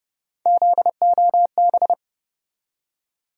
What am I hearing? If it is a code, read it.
Morse code, heard as a keyed tone, about 32 wpm, 700 Hz. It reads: ZOB
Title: USA Call Sign Postfixes - 3 Letters 30wpm